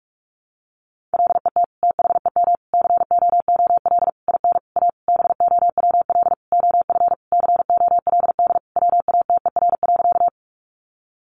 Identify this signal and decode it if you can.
Morse code, heard as a keyed tone, about 44 wpm, 700 Hz. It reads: LET THEM COOL IN A BOWL OF COLD WATER.
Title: Sentences from Top 1200 Words 45wpm